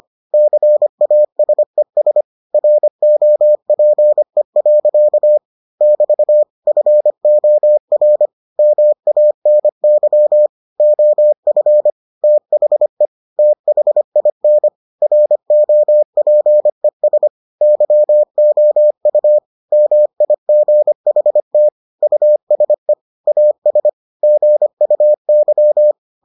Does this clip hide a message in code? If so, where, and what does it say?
Yes: Morse code from 0.3 to 25.9 s, heard as a keyed tone, about 25 wpm, 600 Hz. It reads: CASES ROPE. <BT> FOR MANY OF THE THIN ROPES YOU MIGHT USE AS GUY